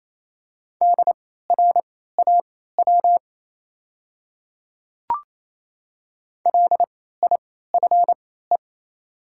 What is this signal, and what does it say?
Morse code, heard as a keyed tone, about 28 wpm, spaced as for 16 wpm, 690 Hz. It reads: DRAW LIFE
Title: 3x Character Spacing - Top 500 Words - 28wpm